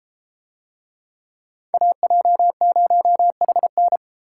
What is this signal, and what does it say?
Morse code, heard as a keyed tone, about 32 wpm, 700 Hz. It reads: AJ0HN